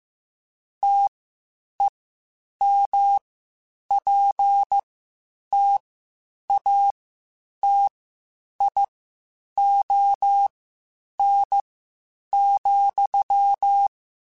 Morse code, heard as a keyed tone, about 15 wpm, spaced as for 8 wpm, 800 Hz. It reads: TEMPTATION,
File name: Spaced / Chuckles Spaced-mp3s / Temptation sp @13